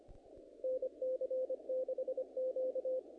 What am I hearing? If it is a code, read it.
Morse code, heard as a keyed tone, about 25 wpm, 520 Hz. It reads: NC6Q
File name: contests / naqp-cw-k5tr-op-k5pi- -2000